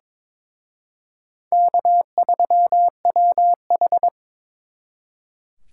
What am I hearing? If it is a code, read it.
Morse code, heard as a keyed tone, about 25 wpm, 700 Hz. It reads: K3WH
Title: Speed Racing - USA 1x2 Call Signs 22wpm